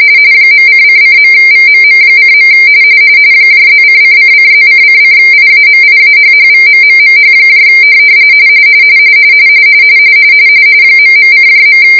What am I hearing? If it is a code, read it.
RTTY, about 45.45 baud. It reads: .:::.VMM:. +::::+:::IAMIAHHVMMMH ..:.